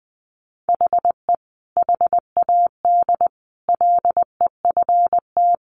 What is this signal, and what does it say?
Morse code, heard as a keyed tone, about 20 wpm, 700 Hz. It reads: HE HAD LEFT
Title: Sentences from Top 1500 Words 20wpm